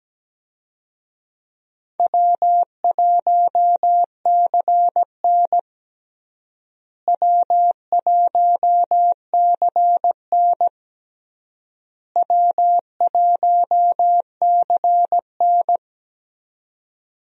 Morse code, heard as a keyed tone, about 17 wpm, 700 Hz. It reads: W1CN W1CN W1CN
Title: USA Call Signs - 4 Characters - Repeated 3x - 17wpm